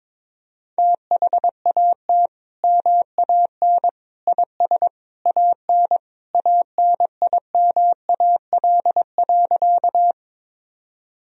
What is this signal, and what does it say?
Morse code, heard as a keyed tone, about 22 wpm, 700 Hz. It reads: THAT MAN IS AN ANIMAL.
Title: Sentences from Top 1000 Words - Review Part 1 - 22wpm